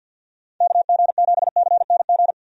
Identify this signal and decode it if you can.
Morse code, heard as a keyed tone, about 50 wpm, 700 Hz. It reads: KG7CNG
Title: USA Call Signs - 6 Characters - 50wpm